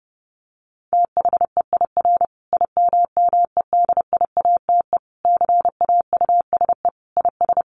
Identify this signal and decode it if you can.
Morse code, heard as a keyed tone, about 30 wpm, 700 Hz. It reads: THEIR IMMEDIATE CAUSE IS
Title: Sentences from Top 1200 Words 30wpm